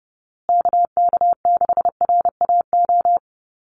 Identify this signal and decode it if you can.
Morse code, heard as a keyed tone, about 30 wpm, 700 Hz. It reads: KK6RAO
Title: USA Call Signs Repeated 3x 30wpm